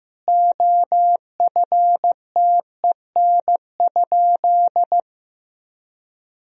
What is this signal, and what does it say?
Morse code, heard as a keyed tone, about 16 wpm, 700 Hz. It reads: OFTEN?
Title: Sentences from Top 400 Words and Abbreviations 15wpm